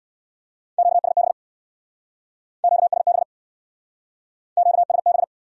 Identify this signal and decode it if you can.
Morse code, heard as a keyed tone, about 60 wpm, 700 Hz. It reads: QSB QSB QSB
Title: Speed-Racing Q-Codes - 50wpm